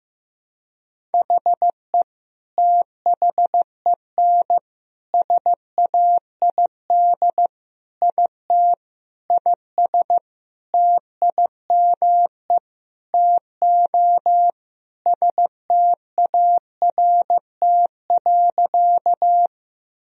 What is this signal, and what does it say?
Morse code, heard as a keyed tone, about 15 wpm, 700 Hz. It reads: HE THEN SAID IT IS TIME TO START.
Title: Sentences from Top 1000 Words - Review Part 1 - 15wpm